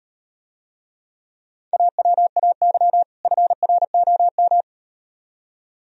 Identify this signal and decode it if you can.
Morse code, heard as a keyed tone, about 38 wpm, 700 Hz. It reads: AWAY FROM